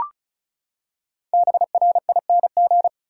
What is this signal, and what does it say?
Morse code, heard as a keyed tone, about 34 wpm, 700 Hz. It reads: BRING
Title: Five Letter Words - 35wpm